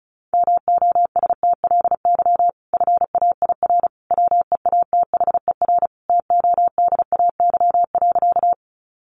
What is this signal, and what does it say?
Morse code, heard as a keyed tone, about 34 wpm, 700 Hz. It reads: MOSTLY FAIR WEATHER TODAY.
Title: Sentences from Top 1300 Words 35wpm